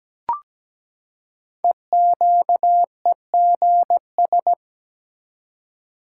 Morse code, heard as a keyed tone, about 17 wpm, 700 Hz. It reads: EQEGS
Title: Code Groups - 5 Letters - 17wpm